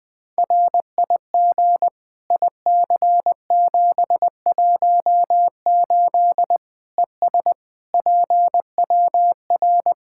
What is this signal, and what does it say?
Morse code, heard as a keyed tone, about 21 wpm, 700 Hz. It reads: RIG IC718 ES PWR
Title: Practice QSO - Exchange 2 - Info - 20wpm